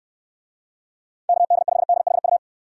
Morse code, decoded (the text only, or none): KD5DHU